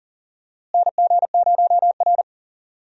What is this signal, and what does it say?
Morse code, heard as a keyed tone, about 40 wpm, 700 Hz. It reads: NG0R
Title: USA Call Signs - 4 Characters - Repeated 2x - 40wpm